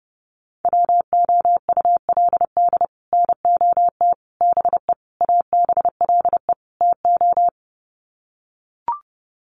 Morse code, heard as a keyed tone, about 30 wpm, 700 Hz. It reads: WOULD NOT BE ABLE TO
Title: Sets of 5 Words 30wpm